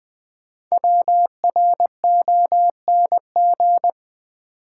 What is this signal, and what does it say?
Morse code, heard as a keyed tone, about 20 wpm, 700 Hz. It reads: WRONG
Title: Sets of 5 Words Encore - New Word Drill - 20wpm